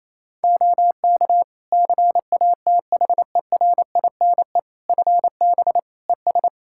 Morse code, heard as a keyed tone, about 28 wpm, 700 Hz. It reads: OK CATHERINE FB ES
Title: Practice QSO - Exchange 2 - Info - 28wpm